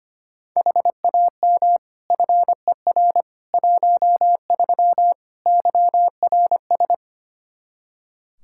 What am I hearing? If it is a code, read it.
Morse code, heard as a keyed tone, about 25 wpm, 700 Hz. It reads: HAM FER 13 YRS
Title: Practice QSO - Exchange 3 - More Info - 25wpm